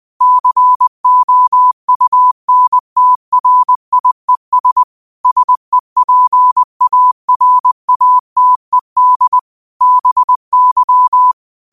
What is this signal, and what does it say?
Morse code, heard as a keyed tone, about 20 wpm, 1000 Hz. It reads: COUNTRIES SEPARATED BY